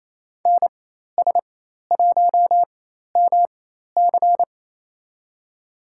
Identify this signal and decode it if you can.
Morse code, heard as a keyed tone, about 28 wpm, spaced as for 13 wpm, 700 Hz. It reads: NS1MC
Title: USA Call Signs - 5 Characters - 4x Character Spacing - 28wpm